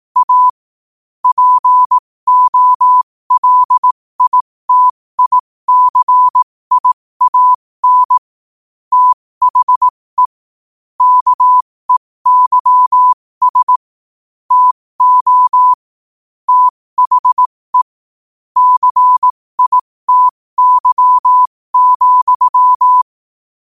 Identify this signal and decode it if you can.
Morse code, heard as a keyed tone, about 18 wpm, spaced as for 15 wpm, 1000 Hz. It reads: A POLITICIAN THE KEYS TO THE CITY,
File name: qotdfiles / 15wpm